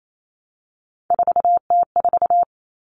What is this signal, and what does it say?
Morse code, heard as a keyed tone, about 28 wpm, 700 Hz. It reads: 4T4
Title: Common Cut Numbers - 3-Digit 28wpm